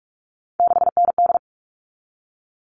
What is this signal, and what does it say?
Morse code, heard as a keyed tone, about 44 wpm, 700 Hz. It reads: 6ND
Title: All Cut Numbers - 3-Digit 45wpm